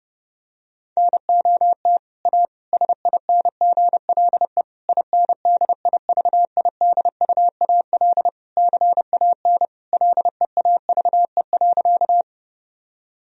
Morse code, heard as a keyed tone, about 30 wpm, 700 Hz. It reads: NOT A SINGLE INDIVIDUAL CAN LEAVE.
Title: Sentences from Top 1000 Words - Review Part 4 - 30wpm